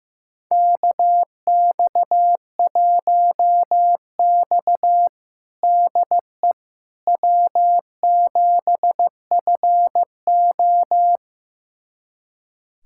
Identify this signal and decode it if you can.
Morse code, heard as a keyed tone, about 16 wpm, 700 Hz. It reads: KX1X DE W7FO
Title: USA Callsign DE Callsign - 4 Characters - 15wpm